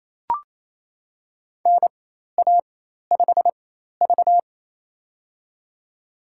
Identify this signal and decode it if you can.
Morse code, heard as a keyed tone, about 27 wpm, spaced as for 13 wpm, 700 Hz. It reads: NA5V
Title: USA Call Signs - 4 Characters - 4x Character Spacing - 28wpm